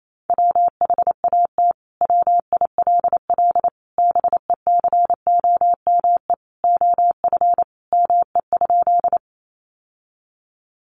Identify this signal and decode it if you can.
Morse code, heard as a keyed tone, about 28 wpm, 700 Hz. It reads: WHAT WILL BECOME OF ME?
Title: Sentences from Top 600 Words 28wpm